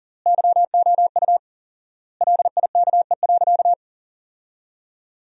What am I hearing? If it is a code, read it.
Morse code, heard as a keyed tone, about 40 wpm, 700 Hz. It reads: YOU LIKE.
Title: 4x Word-Spacing - Sentences from Top 500 Words - 40wpm